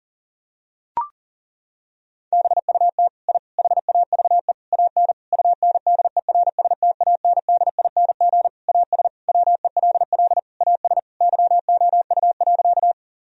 Morse code, heard as a keyed tone, about 40 wpm, 700 Hz. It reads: BUT I HAVE AN UNDERSTANDING AS WELL AS YOU.
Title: Sentences from Top 1000 Words - Review Part 3 - 40wpm